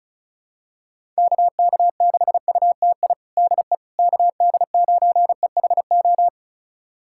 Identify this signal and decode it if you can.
Morse code, heard as a keyed tone, about 36 wpm, 700 Hz. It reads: KK6UTI DE KD9EHO
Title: USA Callsign DE Callsign 35wpm